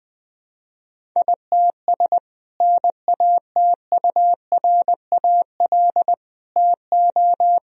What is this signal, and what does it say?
Morse code, heard as a keyed tone, about 21 wpm, 700 Hz. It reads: ITS NATURAL TO
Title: Sentences from Top 1000 Words - Review Part 4 - 20wpm